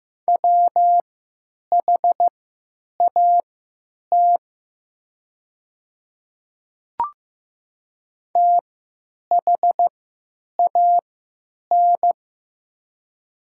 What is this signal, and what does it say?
Morse code, heard as a keyed tone, about 15 wpm, spaced as for 8 wpm, 700 Hz. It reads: WHAT THAN
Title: 3x Character Spacing - Top 100 Words - 15wpm